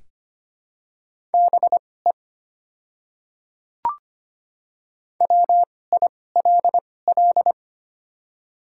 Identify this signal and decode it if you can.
Morse code, heard as a keyed tone, about 25 wpm, spaced as for 18 wpm, 700 Hz. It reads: BE WILL
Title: 2x Character Spacing - Top 200 Words - 25wpm